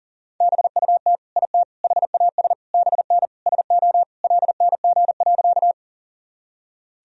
Morse code, heard as a keyed tone, about 40 wpm, 700 Hz. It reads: BUT IT HAS BN SO LNG.
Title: Sentences from Top 100 Words and Abbreviations 40wpm